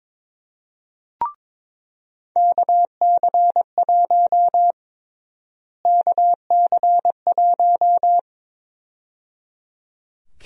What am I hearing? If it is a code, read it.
Morse code, heard as a keyed tone, about 22 wpm, 700 Hz. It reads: KC1 KC1
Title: USA Call Sign Prefixes - Repeated 2x - 22wpm